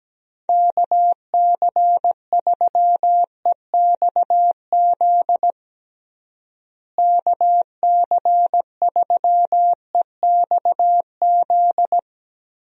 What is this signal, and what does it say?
Morse code, heard as a keyed tone, about 17 wpm, 700 Hz. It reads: KC3EXZ KC3EXZ
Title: USA Call Signs - 6 Characters - Repeated 3x - 17wpm